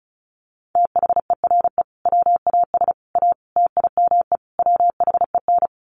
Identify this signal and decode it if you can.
Morse code, heard as a keyed tone, about 34 wpm, 700 Hz. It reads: THERE WAS A TIME WHEN
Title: Sets of 5 Words 35wpm